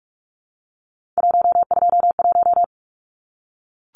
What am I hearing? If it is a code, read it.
Morse code, heard as a keyed tone, about 44 wpm, 700 Hz. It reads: 121
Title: Numbers - 3-Digit 45wpm